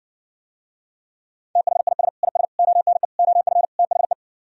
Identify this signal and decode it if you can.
Morse code, heard as a keyed tone, about 70 wpm, 700 Hz. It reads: THIS IS ONE OF THE